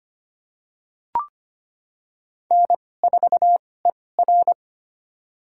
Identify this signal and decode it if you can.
Morse code, heard as a keyed tone, about 25 wpm, spaced as for 18 wpm, 700 Hz. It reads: N4ER
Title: USA Call Signs - 4 Characters - 2x Character Spacing - 25wpm